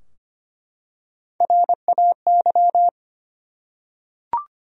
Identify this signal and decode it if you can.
Morse code, heard as a keyed tone, about 25 wpm, 700 Hz. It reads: RAY
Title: USA Common Names - Max 4 Letters - 25wpm